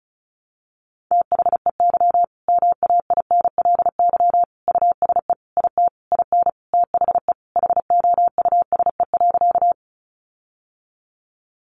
Morse code, heard as a keyed tone, about 34 wpm, 700 Hz. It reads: THEY MAINLY USE IT IN THE HOUSE.
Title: Sentences from Top 1400 Words 35wpm